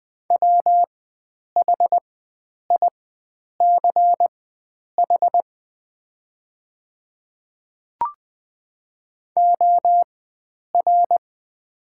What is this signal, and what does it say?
Morse code, heard as a keyed tone, about 20 wpm, spaced as for 9 wpm, 700 Hz. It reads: WHICH OR
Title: 4x Character Spacing - Top 100 Words - 20wpm